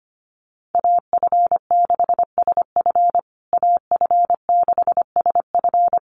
Morse code, heard as a keyed tone, about 25 wpm, 700 Hz. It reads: AF6SF AF6SF
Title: USA Call Signs Repeated 3x 25wpm